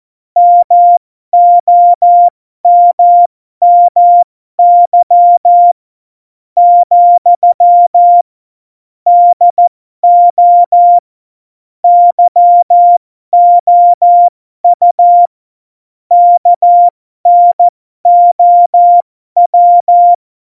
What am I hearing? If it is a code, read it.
Morse code, heard as a keyed tone, about 14 wpm, spaced as for 12 wpm, 700 Hz. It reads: MOMMY , DO YOU KNOW
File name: Morsekurs / Witze